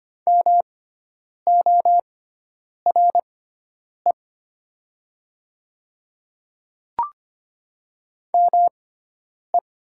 Morse code, heard as a keyed tone, about 25 wpm, spaced as for 9 wpm, 700 Hz. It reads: MORE ME
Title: 6x Character Spacing - Top 200 Words - 25wpm